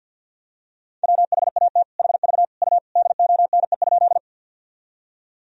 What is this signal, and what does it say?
Morse code, heard as a keyed tone, about 52 wpm, 700 Hz. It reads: WHAT HV U DONE?
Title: Sentences from Top 500 Words and Abbreviations 50wpm